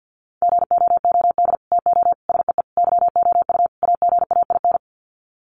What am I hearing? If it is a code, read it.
Morse code, heard as a keyed tone, about 50 wpm, 700 Hz. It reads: GOOD TO SEE YOU AGAIN
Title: Sets of 5 Words Encore2 50wpm